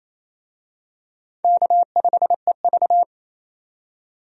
Morse code, heard as a keyed tone, about 28 wpm, 700 Hz. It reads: K5EV
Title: USA Call Signs - 4 Characters - 28wpm